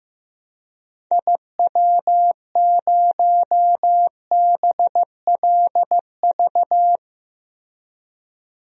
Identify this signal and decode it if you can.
Morse code, heard as a keyed tone, about 16 wpm, 700 Hz. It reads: IW0BLV
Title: DXCC-100 Call Signs - 6 Characters - Repeated 2x - 15wpm